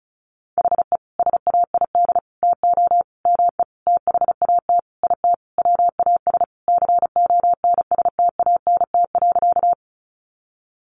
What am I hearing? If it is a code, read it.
Morse code, heard as a keyed tone, about 34 wpm, 700 Hz. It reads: HE SAID TO ME THAT IT WAS CONSTANT.
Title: Sentences from Top 1500 Words 35wpm